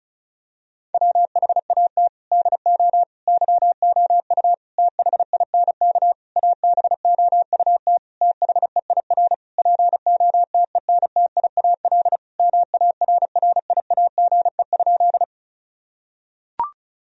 Morse code, heard as a keyed tone, about 36 wpm, 700 Hz. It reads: WHAT DO YOU THINK ABOUT THEIR POTENTIAL MARRIAGE?
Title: Sentences from Top 1000 Words - Review Part 4 - 35wpm